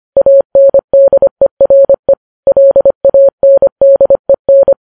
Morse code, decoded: ANDERE LANDEN